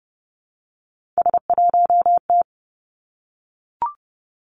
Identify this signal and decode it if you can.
Morse code, heard as a keyed tone, about 30 wpm, 700 Hz. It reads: S1T
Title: Three Challenging Letter-Number Combinations 30wpm